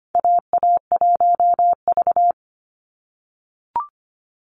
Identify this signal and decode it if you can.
Morse code, heard as a keyed tone, about 25 wpm, 700 Hz. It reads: AA1V
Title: Call Signs - 25wpm